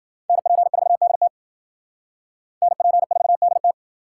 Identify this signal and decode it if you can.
Morse code, heard as a keyed tone, about 51 wpm, 700 Hz. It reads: NP4DT NP4DT